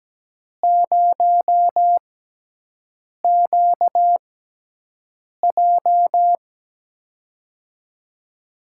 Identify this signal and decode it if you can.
Morse code, heard as a keyed tone, about 17 wpm, spaced as for 6 wpm, 700 Hz. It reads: 0QJ
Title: USA Call Signs - 4 Characters - 6x Character Spacing - 17wpm